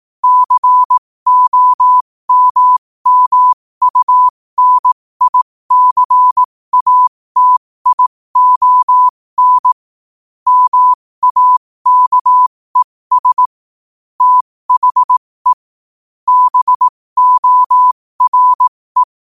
Morse code, heard as a keyed tone, about 18 wpm, spaced as for 15 wpm, 1000 Hz. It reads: COMMUNICATION MAKES THE BORE
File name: qotdfiles / 15wpm